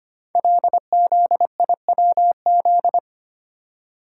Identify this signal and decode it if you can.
Morse code, heard as a keyed tone, about 25 wpm, 700 Hz. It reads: LZIWZ
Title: Code Groups - 5 Letters - 25wpm